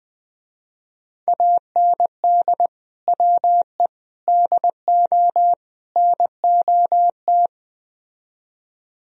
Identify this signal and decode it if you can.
Morse code, heard as a keyed tone, about 20 wpm, 700 Hz. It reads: AND WE DO NOT
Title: Speed-Racing - Sets of 4 Words 17wpm